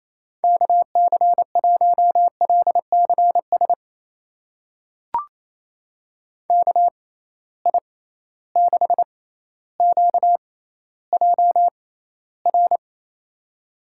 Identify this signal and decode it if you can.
Morse code, heard as a keyed tone, about 28 wpm, spaced as for 14 wpm, 700 Hz. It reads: KC1LCS KI6QJR
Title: USA Call Signs - 6 Characters - 6x Character Spacing - 28wpm